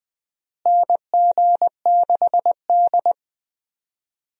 Morse code, heard as a keyed tone, about 20 wpm, 700 Hz. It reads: NG6D
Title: USA Call Signs - 4 Characters - Repeated 3x - 20wpm